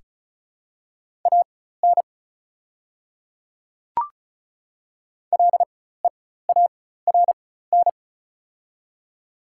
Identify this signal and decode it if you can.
Morse code, heard as a keyed tone, about 34 wpm, spaced as for 16 wpm, 700 Hz. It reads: AN LEARN
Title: 4x Character Spacing - Top 300 Words - 35wpm